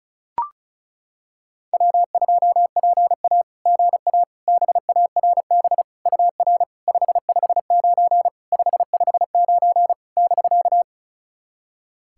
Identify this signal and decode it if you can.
Morse code, heard as a keyed tone, about 36 wpm, 690 Hz. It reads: W2PA GA BARB UR 559 559 <BK>
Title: SOTA QSO Practice 35wpm